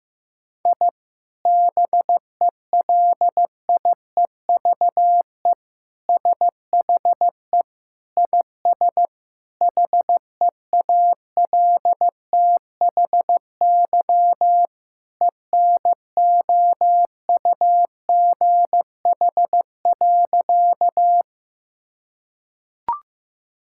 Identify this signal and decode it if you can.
Morse code, heard as a keyed tone, about 15 wpm, 700 Hz. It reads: I BELIEVE SHE IS HEALTHY ENOUGH.
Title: Sentences from Top 1000 Words - Review Part 5 - 15wpm